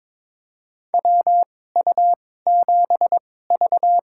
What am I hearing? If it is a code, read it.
Morse code, heard as a keyed tone, about 21 wpm, spaced as for 16 wpm, 700 Hz. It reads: WU7V
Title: USA Call Signs - 4 Characters - 2x Character Spacing - 22wpm